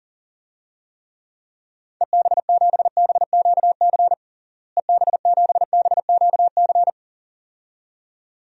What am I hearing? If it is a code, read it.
Morse code, heard as a keyed tone, about 40 wpm, 700 Hz. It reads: EB7BQC EB7BQC
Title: DXCC-100 Call Signs - 6 Characters - Repeated 2x - 40wpm